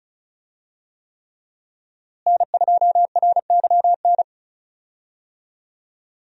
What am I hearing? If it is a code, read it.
Morse code, heard as a keyed tone, about 36 wpm, 700 Hz. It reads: N2RYN
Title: USA Call Signs - 5 Characters - Repeated 3x - 35wpm